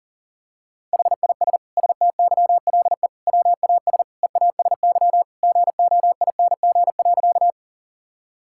Morse code, heard as a keyed tone, about 40 wpm, 700 Hz. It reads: HIS STYLE WAS EASY GOING.
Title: Sentences from Top 1000 Words - Review Part 5 - 40wpm